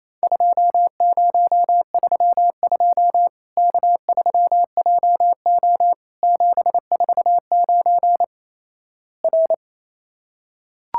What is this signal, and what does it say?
Morse code, heard as a keyed tone, about 28 wpm, 700 Hz. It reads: 2032 K3JO 749 R
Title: Worked All Europe DX Contest - QTC Exchange - 28wpm